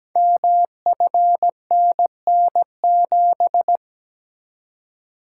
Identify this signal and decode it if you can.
Morse code, heard as a keyed tone, about 17 wpm, 700 Hz. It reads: MFNN7
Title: Code Groups - 5 Letter:Numbers - 17wpm